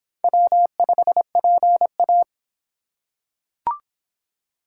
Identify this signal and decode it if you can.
Morse code, heard as a keyed tone, about 26 wpm, 700 Hz. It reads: W5PA